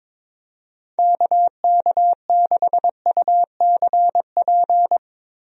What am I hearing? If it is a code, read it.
Morse code, heard as a keyed tone, about 22 wpm, 700 Hz. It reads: KK6UCP
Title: USA Call Signs - 6 Characters - Repeated 2x - 22wpm